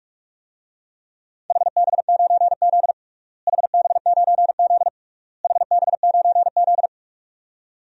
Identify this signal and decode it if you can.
Morse code, heard as a keyed tone, about 48 wpm, 700 Hz. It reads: HB9Z HB9Z HB9Z
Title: DXCC-100 Call Signs - 4 Characters - Repeated 3x - 45wpm